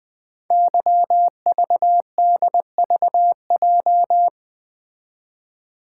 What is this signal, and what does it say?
Morse code, heard as a keyed tone, about 20 wpm, 700 Hz. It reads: YVDVJ
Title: Code Groups - 5 Letters - 20wpm